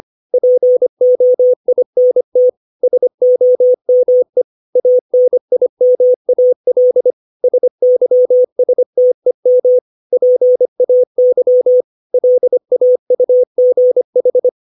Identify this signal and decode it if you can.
Morse code, heard as a keyed tone, about 25 wpm, 500 Hz. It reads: POINT SOME ANIMAL SYSTEM PAY LAUGH
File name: 632 words / 632 Words 25WPM